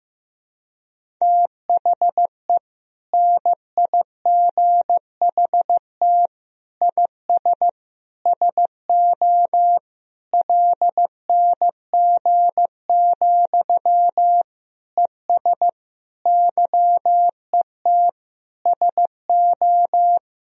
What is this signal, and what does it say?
Morse code, heard as a keyed tone, about 16 wpm, 700 Hz. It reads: THE NIGHT IS SO LNG, ES YET SO
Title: Sentences from Top 500 Words and Abbreviations 15wpm